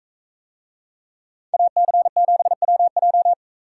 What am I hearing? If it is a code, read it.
Morse code, heard as a keyed tone, about 44 wpm, 700 Hz. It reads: AC7WJ